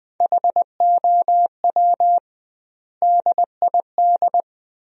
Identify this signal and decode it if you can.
Morse code, heard as a keyed tone, about 20 wpm, 700 Hz. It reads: HOW DID
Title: 2x Word-Spacing - Sentences from Top 100 Words - 20wpm